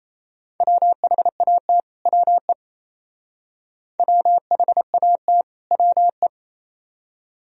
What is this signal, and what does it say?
Morse code, heard as a keyed tone, about 34 wpm, 700 Hz. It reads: WHAT WE WHAT WE
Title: Speed-Racing - Sets of 2 Words 28wpm